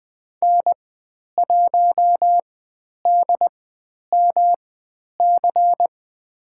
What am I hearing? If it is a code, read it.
Morse code, heard as a keyed tone, about 21 wpm, spaced as for 10 wpm, 700 Hz. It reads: N1DMC
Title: 109 Lesson 18g - Letter m - Call Signs 20wpm